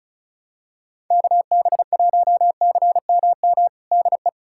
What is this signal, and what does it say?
Morse code, heard as a keyed tone, about 36 wpm, 700 Hz. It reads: KB1CMM DE
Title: USA Callsign DE Callsign - 6 Characters - 35wpm